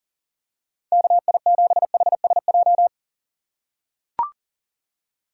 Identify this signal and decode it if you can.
Morse code, heard as a keyed tone, about 40 wpm, 700 Hz. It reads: KI7HSJ
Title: USA Call Signs - 6 Characters - 6x Character Spacing - 40wpm